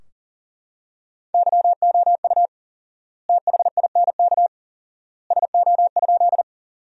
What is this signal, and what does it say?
Morse code, heard as a keyed tone, about 40 wpm, 700 Hz. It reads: YOU THINK SO?
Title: 4x Word-Spacing - Sentences from Top 200 Words - 40wpm